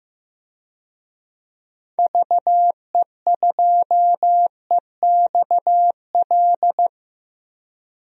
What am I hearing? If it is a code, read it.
Morse code, heard as a keyed tone, about 15 wpm, 700 Hz. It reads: VE2EXL
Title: DXCC-100 Call Signs - 6 Characters - Repeated 3x - 15wpm